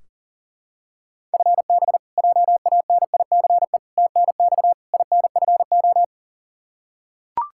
Morse code, heard as a keyed tone, about 37 wpm, 700 Hz. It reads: FB JANICE TNX INFO
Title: K1USN SST QSO Practice 40wpm